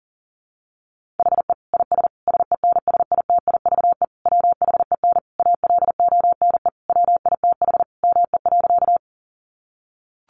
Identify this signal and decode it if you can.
Morse code, heard as a keyed tone, about 40 wpm, 700 Hz. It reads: HE IS SENSITIVE WHEN ALONE WITH ME.
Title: Sentences from Top 1400 Words 40wpm